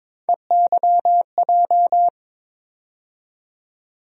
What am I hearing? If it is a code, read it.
Morse code, heard as a keyed tone, about 22 wpm, 700 Hz. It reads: EYJ
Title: USA Call Sign Postfixes - 3 Letters 22wpm